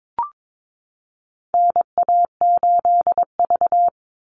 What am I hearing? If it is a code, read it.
Morse code, heard as a keyed tone, about 22 wpm, 700 Hz. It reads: NA8V
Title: Call Signs - 22wpm